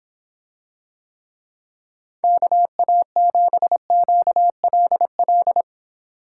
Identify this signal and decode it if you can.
Morse code, heard as a keyed tone, about 26 wpm, 700 Hz. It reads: KA7QLL